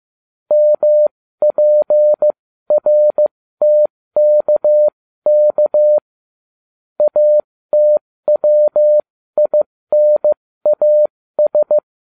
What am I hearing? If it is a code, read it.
Morse code, heard as a keyed tone, about 15 wpm, spaced as for 13 wpm, 600 Hz. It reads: MPRTKK ATWINAS